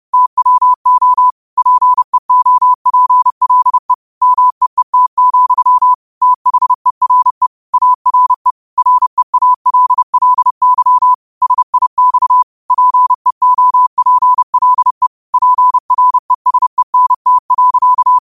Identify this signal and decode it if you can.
Morse code, heard as a keyed tone, about 30 wpm, 1000 Hz. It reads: TWO PEOPLE MEET, THERE ARE REALLY SIX PEOPLE PRESENT.